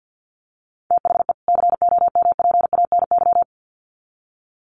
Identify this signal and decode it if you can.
Morse code, heard as a keyed tone, about 50 wpm, 700 Hz. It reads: THE COMPANY
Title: Sets of 2 Words 50wpm